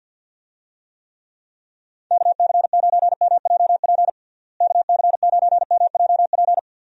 Morse code, heard as a keyed tone, about 50 wpm, 700 Hz. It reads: KC9MJP KC9MJP
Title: USA Call Signs - 6 Characters - Repeated 2x - 50wpm